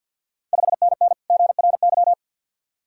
Morse code, heard as a keyed tone, about 52 wpm, 690 Hz. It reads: 5NN GRY
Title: Ontario QSO Party Exchange 50wpm